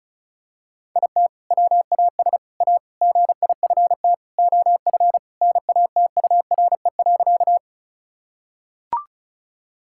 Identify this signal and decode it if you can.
Morse code, heard as a keyed tone, about 34 wpm, 700 Hz. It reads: IT WAS A GIFT OF NATURE.
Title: Sentences from Top 1000 Words - Review Part 6 - 35wpm